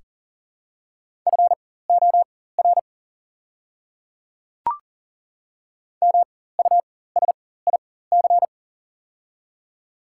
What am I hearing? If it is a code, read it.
Morse code, heard as a keyed tone, about 40 wpm, spaced as for 19 wpm, 700 Hz. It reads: FOR MUSIC
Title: 4x Character Spacing - Top 300 Words - 40wpm